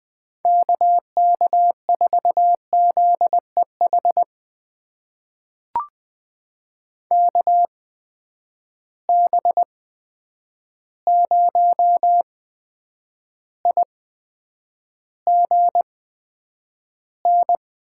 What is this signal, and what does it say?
Morse code, heard as a keyed tone, about 20 wpm, spaced as for 9 wpm, 700 Hz. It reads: KK4ZEH KB0IGN
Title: USA Call Signs - 6 Characters - 8x Character Spacing - 20wpm